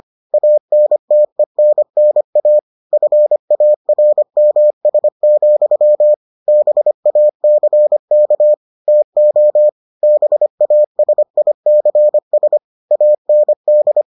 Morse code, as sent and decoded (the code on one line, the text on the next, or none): .- -. - . -. -. .- / ..-. .- .-. -- ... --..-- / -... .- -.-. -.- / - --- / -... .- ... .. -.-. ... / .- -. -..
ANTENNA FARMS, BACK TO BASICS AND